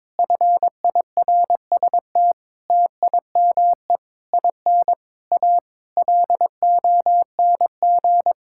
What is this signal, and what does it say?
Morse code, heard as a keyed tone, about 22 wpm, 700 Hz. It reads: FIRST TIME IN A LONG
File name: practice / 22wpm / Sets of 8 Words - 22wpm